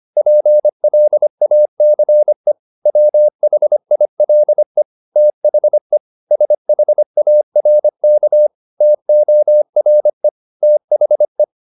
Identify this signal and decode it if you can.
Morse code, heard as a keyed tone, about 25 wpm, 600 Hz. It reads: PLACE WHILE THE SHARK TORE THE